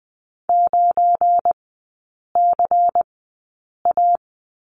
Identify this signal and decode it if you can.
Morse code, heard as a keyed tone, about 20 wpm, spaced as for 8 wpm, 700 Hz. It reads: 9CA
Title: 205 Lesson 33e - Letter j - Three Characters 20wpm